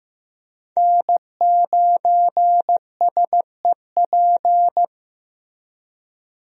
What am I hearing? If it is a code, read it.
Morse code, heard as a keyed tone, about 15 wpm, 700 Hz. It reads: N9SEP
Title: USA Call Signs - 5 Characters - Repeated 2x - 15wpm